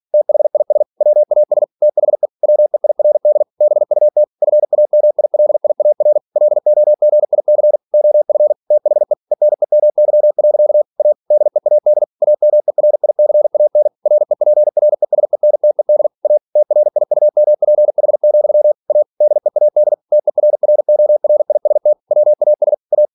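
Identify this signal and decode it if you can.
Morse code, heard as a keyed tone, about 46 wpm, 600 Hz. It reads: THIS WAS THE WEIRD BUT FAMILIAR LOGIC OF THE ENEMY. A DEAD AMERICAN REPRESENTED A TRIUMPH, A DEAD TERRORIST WAS A
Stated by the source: Practice text from an English novel sent at 47 WPM.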